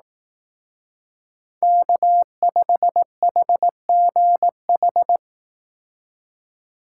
Morse code, heard as a keyed tone, about 18 wpm, 700 Hz. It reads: K5HGH
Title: Speed Racing - USA Call Signs 15wpm